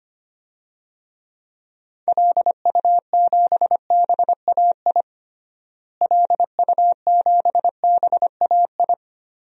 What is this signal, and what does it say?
Morse code, heard as a keyed tone, about 25 wpm, 700 Hz. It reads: LU7BAI LU7BAI
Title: DXCC-100 Call Signs - 6 Characters - Repeated 2x - 25wpm